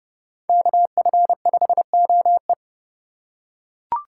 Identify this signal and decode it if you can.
Morse code, heard as a keyed tone, about 29 wpm, 700 Hz. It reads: KF5OE